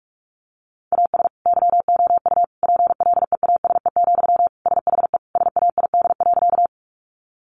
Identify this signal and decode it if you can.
Morse code, heard as a keyed tone, about 46 wpm, 700 Hz. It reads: AS YOU PLEASE, SHE SAID.
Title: Sentences from Top 1500 Words 45wpm